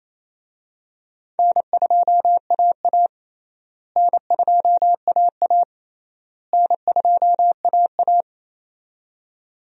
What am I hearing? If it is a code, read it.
Morse code, heard as a keyed tone, about 28 wpm, 700 Hz. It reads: N2AA N2AA N2AA
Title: USA Call Signs - 4 Characters - Repeated 3x - 28wpm